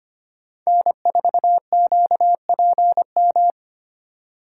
Morse code, heard as a keyed tone, about 33 wpm, 700 Hz. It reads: N4QPM